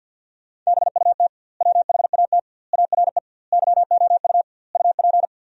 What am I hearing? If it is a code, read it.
Morse code, heard as a keyed tone, about 50 wpm, 700 Hz. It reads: BUT WHAT ARE YOU UP
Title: 2x Word-Spacing - Sentences from Top 100 Words - 50wpm